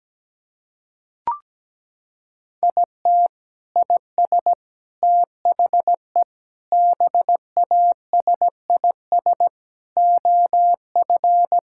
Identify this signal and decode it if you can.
Morse code, heard as a keyed tone, about 17 wpm, 700 Hz. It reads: IT IS THE BASIS OF
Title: Sentences from Top 1000 Words - Review Part 5 - 17wpm